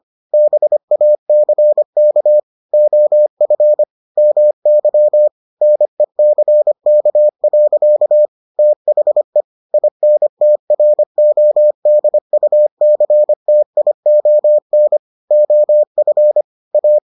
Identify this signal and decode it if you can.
Morse code, heard as a keyed tone, about 25 wpm, 600 Hz. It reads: BACK OF MY NECK. THE INTRODUCTION OF A